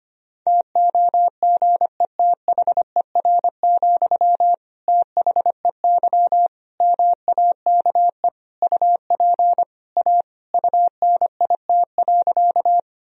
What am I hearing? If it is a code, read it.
Morse code, heard as a keyed tone, about 25 wpm, 700 Hz. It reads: TOGETHER, THEY MAKE UP A UNIT.
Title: Sentences from Top 1000 Words - Review Part 2 - 25wpm